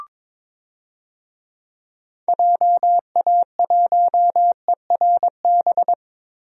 Morse code, heard as a keyed tone, about 22 wpm, 700 Hz. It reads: JA1ERB
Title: DXCC-100 Call Signs - 6 Characters - Repeated 3x - 22wpm